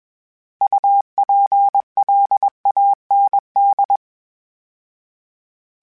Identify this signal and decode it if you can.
Morse code, heard as a keyed tone, about 21 wpm, 800 Hz. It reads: UPLAND
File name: fromSMs / FromSMs mp3s / Cities@20